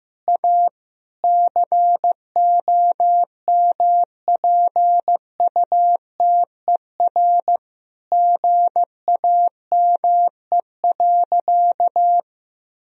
Morse code, heard as a keyed tone, about 15 wpm, 700 Hz. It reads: A COMPUTER GAME.
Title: Sentences from Top 1000 Words - Review Part 3 - 15wpm